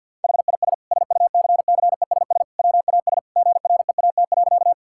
Morse code, heard as a keyed tone, about 50 wpm, 700 Hz. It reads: HIS SUCCESS WAS GREAT.
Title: Sentences from Top 1000 Words - Review Part 6 - 50wpm